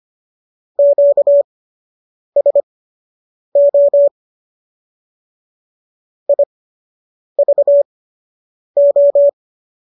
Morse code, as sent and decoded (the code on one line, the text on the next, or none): --.- ... --- / .. ...- ---
QSO IVO